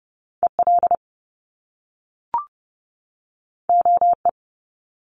Morse code, heard as a keyed tone, about 30 wpm, 700 Hz. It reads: EL OE
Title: Top Two Letter Combinations in Words 30wpm